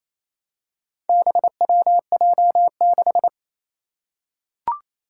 Morse code, heard as a keyed tone, about 28 wpm, 700 Hz. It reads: BWJ6
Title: Four Challenging Letter-Number Combinations 28wpm